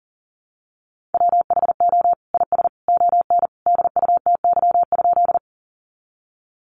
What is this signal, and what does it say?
Morse code, heard as a keyed tone, about 39 wpm, 700 Hz. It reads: WHO IS ON DUTY?
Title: Sentences from Top 1500 Words 40wpm